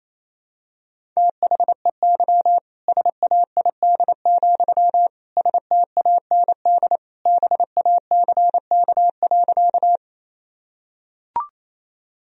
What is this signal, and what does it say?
Morse code, heard as a keyed tone, about 28 wpm, 700 Hz. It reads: THEY SAID, STAND BACK.
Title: Sentences from Top 1000 Words - Review Part 1 - 28wpm